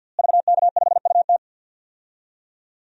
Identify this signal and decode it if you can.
Morse code, heard as a keyed tone, about 47 wpm, 700 Hz. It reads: VK5UT
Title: DXCC-100 Call Signs - 5 Characters - Repeated 2x - 50wpm